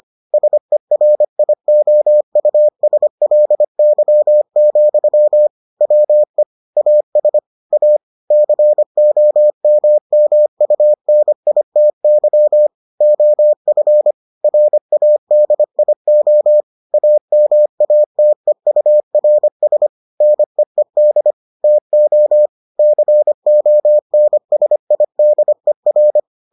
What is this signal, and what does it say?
Morse code, heard as a keyed tone, about 25 wpm, 600 Hz. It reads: SERIOUSLY, WE AS A COMMUNITY OF RADIO AMATEURS NEED TO CONSIDER